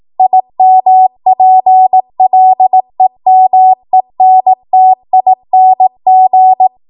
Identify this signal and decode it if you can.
Morse code, heard as a keyed tone, about 18 wpm, 750 Hz. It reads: IMPLEMENTING
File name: Morse / 18WPM